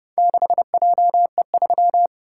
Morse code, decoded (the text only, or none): 6JE3